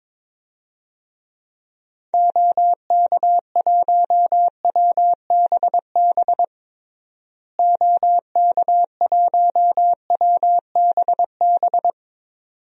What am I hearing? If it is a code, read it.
Morse code, heard as a keyed tone, about 22 wpm, 700 Hz. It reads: OK1WBB OK1WBB
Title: DXCC-100 Call Signs - 6 Characters - Repeated 2x - 22wpm